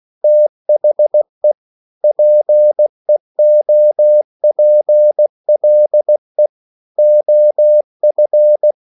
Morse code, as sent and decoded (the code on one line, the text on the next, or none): - .... / .--. --- .--. .-.. / --- ..-.
THE PEOPLE OF